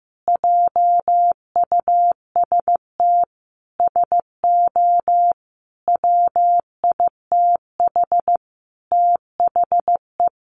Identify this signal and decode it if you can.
Morse code, heard as a keyed tone, about 15 wpm, 700 Hz. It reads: JUST SO WITH THE
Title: Sentences from Top 200 Words 15wpm